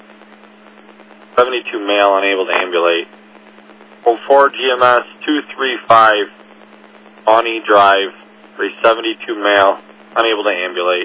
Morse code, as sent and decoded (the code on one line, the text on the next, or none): none